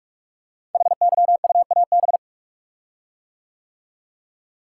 Morse code, heard as a keyed tone, about 44 wpm, 700 Hz. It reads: HYUAB
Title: Code Groups - 5 Letter:Numbers - 45wpm